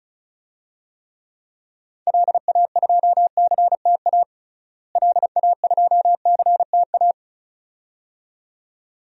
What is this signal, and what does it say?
Morse code, heard as a keyed tone, about 34 wpm, 700 Hz. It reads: LA2CTA LA2CTA
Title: DXCC-100 Call Signs - 6 Characters - Repeated 2x - 35wpm